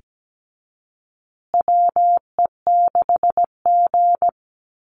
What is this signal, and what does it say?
Morse code, heard as a keyed tone, about 17 wpm, 700 Hz. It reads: WE6G
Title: USA Call Signs Repeated 3x 17wpm